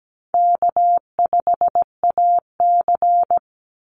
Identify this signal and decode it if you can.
Morse code, heard as a keyed tone, about 17 wpm, 700 Hz. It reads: K5AC
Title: USA Call Signs 17wpm